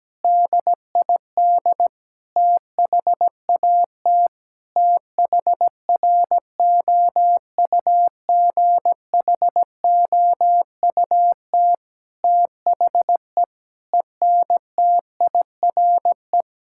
Morse code, heard as a keyed tone, about 17 wpm, 700 Hz. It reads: DID THAT THROUGHOUT THE ENTIRE
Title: Sentences from Top 1000 Words - Review Part 5 - 17wpm